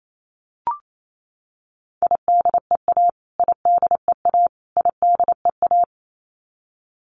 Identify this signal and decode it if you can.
Morse code, heard as a keyed tone, about 28 wpm, 700 Hz. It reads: IDEA IDEA IDEA
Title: Top 500 Words Repeated 28wpm